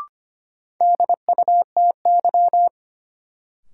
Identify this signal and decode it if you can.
Morse code, heard as a keyed tone, about 25 wpm, 700 Hz. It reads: DUTY
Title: Sentences from Top 1500 Words - New Word Drill - 25wpm